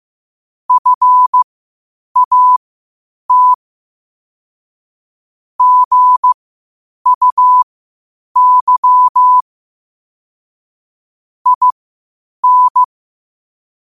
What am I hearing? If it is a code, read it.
Morse code, heard as a keyed tone, about 15 wpm, spaced as for 8 wpm, 1000 Hz. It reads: FAT GUY IN
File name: qotdfiles / 8wpm